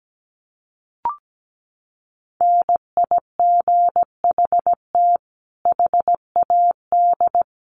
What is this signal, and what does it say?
Morse code, heard as a keyed tone, about 17 wpm, 700 Hz. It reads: NIGHT HAD
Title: Sentences from Top 1300 Words 17wpm